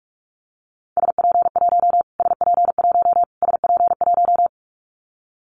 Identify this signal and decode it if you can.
Morse code, heard as a keyed tone, about 44 wpm, 700 Hz. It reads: SP1 SP1 SP1
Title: Callsign Prefixes for Top 25 DXCC - Repeated 3x - 45wpm